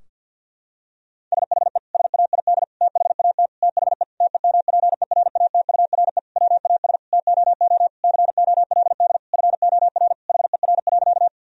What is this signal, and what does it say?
Morse code, heard as a keyed tone, about 50 wpm, 700 Hz. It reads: SHE SAID THAT THE TEMPERATURE WAS TOO COLD FOR HER.
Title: Sentences from Top 1000 Words - Review Part 4 - 50wpm